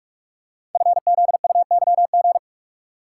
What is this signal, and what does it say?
Morse code, heard as a keyed tone, about 44 wpm, 700 Hz. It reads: FZUYG
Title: Code Groups - 5 Letter:Numbers - 45wpm